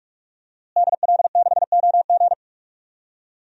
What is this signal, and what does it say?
Morse code, heard as a keyed tone, about 42 wpm, 700 Hz. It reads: DL6OG
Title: DXCC-100 Call Signs - 5 Characters - Repeated 2x - 45wpm